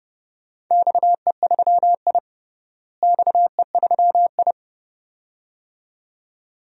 Morse code, heard as a keyed tone, about 32 wpm, 700 Hz. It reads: XE3I XE3I
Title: DXCC-100 Call Signs - 4 Characters - Repeated 2x - 30wpm